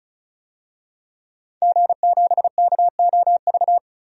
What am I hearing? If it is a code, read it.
Morse code, heard as a keyed tone, about 36 wpm, 700 Hz. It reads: G7KOV
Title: DXCC-100 Call Signs - 5 Characters - 35wpm